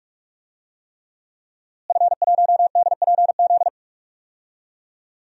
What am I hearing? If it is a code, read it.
Morse code, heard as a keyed tone, about 46 wpm, 700 Hz. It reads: F1DPZ
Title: DXCC-100 Call Signs - 5 Characters - Repeated 3x - 45wpm